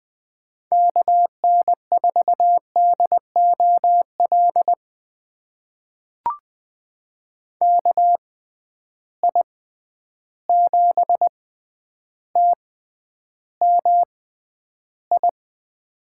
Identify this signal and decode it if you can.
Morse code, heard as a keyed tone, about 20 wpm, spaced as for 10 wpm, 700 Hz. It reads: KN4DOL KI7TMI
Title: USA Call Signs - 6 Characters - 6x Character Spacing - 20wpm